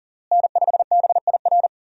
Morse code, heard as a keyed tone, about 37 wpm, 700 Hz. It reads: N5BIR
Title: USA Call Signs - 5 Characters - Repeated 3x - 40wpm